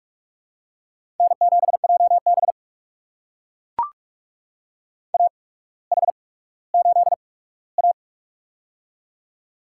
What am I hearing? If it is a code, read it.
Morse code, heard as a keyed tone, about 44 wpm, spaced as for 19 wpm, 690 Hz. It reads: N7JB AH8A
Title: USA Call Signs - 4 Characters - 8x Character Spacing - 45wpm